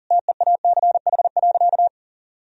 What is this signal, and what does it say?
Morse code, heard as a keyed tone, about 40 wpm, 700 Hz. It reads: TEACH.
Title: Sentences from Top 1000 Words - Review Part 2 - 40wpm